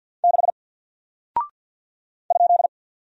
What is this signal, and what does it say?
Morse code, heard as a keyed tone, about 49 wpm, 700 Hz. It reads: 6 ?
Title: Single Character (Letter:Number:Punctuation:Prosign) 50wpm